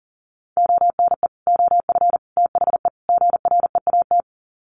Morse code, heard as a keyed tone, about 40 wpm, 700 Hz. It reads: ONE OF THE GREAT
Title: Sets of 4 Words 40wpm